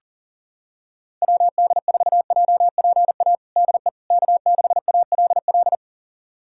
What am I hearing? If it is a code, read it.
Morse code, heard as a keyed tone, about 40 wpm, 700 Hz. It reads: WD4JPA DE K6ALL
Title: USA Callsign DE Callsign 40wpm